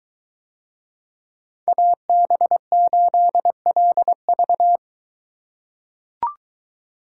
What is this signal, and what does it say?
Morse code, heard as a keyed tone, about 22 wpm, 700 Hz. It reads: AB8LV